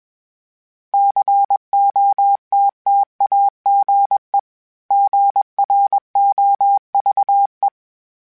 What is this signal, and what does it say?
Morse code, heard as a keyed tone, about 21 wpm, 800 Hz. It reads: COTTAGE GROVE
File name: fromSMs / FromSMs mp3s / Cities@20